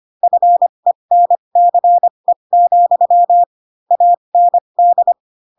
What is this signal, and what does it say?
Morse code, heard as a keyed tone, about 25 wpm, spaced as for 22 wpm, 700 Hz. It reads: FENCE, AND
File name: Pinocchio / Pinocchio ch2 18 25 ik1hge